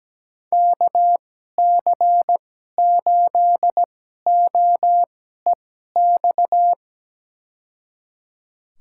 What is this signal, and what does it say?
Morse code, heard as a keyed tone, about 17 wpm, spaced as for 12 wpm, 700 Hz. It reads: KC8OEX
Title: USA Call Signs - 2x Character Spacing - 17wpm